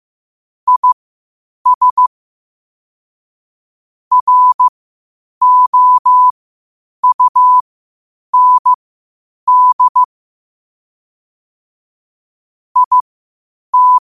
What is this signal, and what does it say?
Morse code, heard as a keyed tone, about 15 wpm, spaced as for 8 wpm, 1000 Hz. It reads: IS ROUND IT
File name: qotdfiles / 8wpm